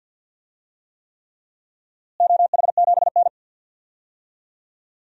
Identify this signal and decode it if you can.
Morse code, heard as a keyed tone, about 51 wpm, 700 Hz. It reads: OH7N
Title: DXCC-100 Call Signs - 4 Characters - Repeated 3x - 50wpm